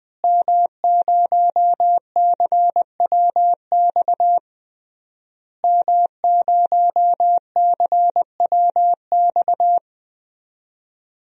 Morse code, heard as a keyed tone, about 20 wpm, 700 Hz. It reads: M0CWX M0CWX
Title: DXCC-100 Call Signs - 5 Characters - Repeated 3x - 20wpm